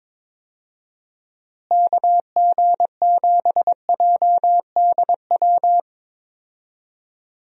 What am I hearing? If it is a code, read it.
Morse code, heard as a keyed tone, about 22 wpm, 700 Hz. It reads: KG7JDW
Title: USA Call Signs - 6 Characters - Repeated 2x - 22wpm